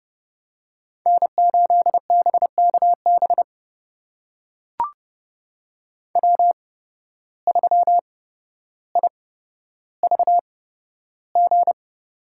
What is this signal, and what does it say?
Morse code, heard as a keyed tone, about 32 wpm, spaced as for 13 wpm, 700 Hz. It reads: N8BKB W3IVG
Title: USA Call Signs - 5 Characters - 8x Character Spacing - 30wpm